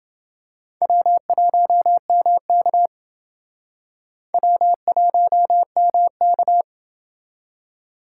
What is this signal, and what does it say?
Morse code, heard as a keyed tone, about 29 wpm, 700 Hz. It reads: W1MK W1MK